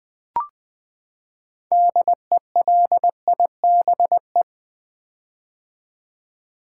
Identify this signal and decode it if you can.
Morse code, heard as a keyed tone, about 20 wpm, 700 Hz. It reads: DELIBE
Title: Build A Word - 6 Letters At A Time - 12-Letter Words - 20wpm